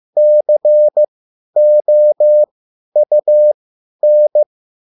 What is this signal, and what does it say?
Morse code, heard as a keyed tone, about 15 wpm, spaced as for 10 wpm, 600 Hz. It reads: COUN